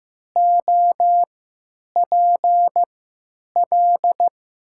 Morse code, heard as a keyed tone, about 15 wpm, spaced as for 9 wpm, 700 Hz. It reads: OPL
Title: 3x Character Spacing - Top 100 Words - 15wpm